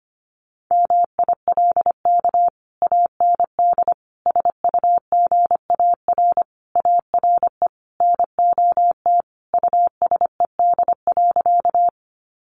Morse code, heard as a keyed tone, about 25 wpm, 700 Hz. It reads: MILK AND SUGAR ARE NOT USED.
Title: Sentences from Top 1500 Words 25wpm